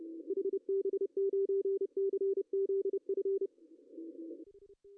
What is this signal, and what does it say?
Morse code, heard as a keyed tone, about 27 wpm, 410 Hz. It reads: HB9CZF
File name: CW / cqww-cw-k5tr